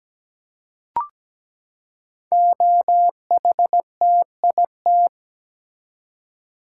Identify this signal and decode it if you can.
Morse code, heard as a keyed tone, about 17 wpm, 700 Hz. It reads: OHTIT
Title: Code Groups - 5 Letters - 17wpm